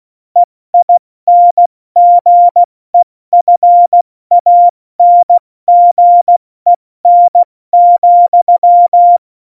Morse code, heard as a keyed tone, about 16 wpm, spaced as for 14 wpm, 700 Hz. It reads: EINGEFANGEN,
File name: Morsekurs / Witze / Witz 27. Februar 2018, 70 F80